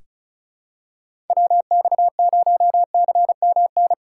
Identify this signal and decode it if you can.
Morse code, heard as a keyed tone, about 36 wpm, 700 Hz. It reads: WX0CMN
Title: USA Call Signs - 6 Characters - 4x Character Spacing - 35wpm